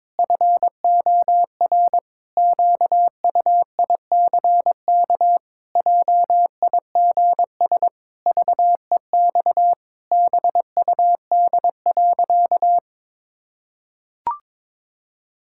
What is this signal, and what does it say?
Morse code, heard as a keyed tone, about 22 wpm, 700 Hz. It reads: FOR QUICK JIGS VEX BUD.
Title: Pangram Challenge v2 - 22wpm